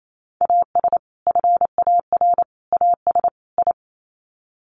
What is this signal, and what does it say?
Morse code, heard as a keyed tone, about 28 wpm, 700 Hz. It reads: AS FAR AS I
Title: Sets of 4 Words 28wpm